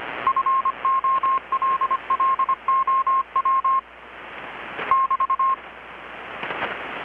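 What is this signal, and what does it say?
Morse code, heard as a keyed tone, about 25 wpm, 1070 Hz. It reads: FOLLOW <BT>
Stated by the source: Morse from KPH's Night of Nights XX maritime radiotelegraphy event, received in Chiapas, Mexico about 3,800 km away on a Kenwood R-600 with a grounded 10 m long-wire antenna.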